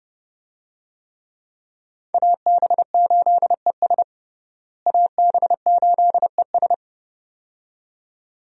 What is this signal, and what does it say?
Morse code, heard as a keyed tone, about 32 wpm, 700 Hz. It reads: AB8ES AB8ES
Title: USA Call Signs - 5 Characters - Repeated 2x - 30wpm